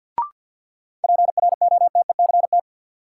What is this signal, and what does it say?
Morse code, heard as a keyed tone, about 48 wpm, 700 Hz. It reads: PROTECT
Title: Sentences from Top 900 Words - New Word Drill - 50wpm